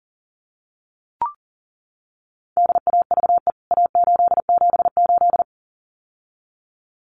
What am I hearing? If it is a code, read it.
Morse code, heard as a keyed tone, about 40 wpm, 700 Hz. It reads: DAVE A878
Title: CWT Exchange 40wpm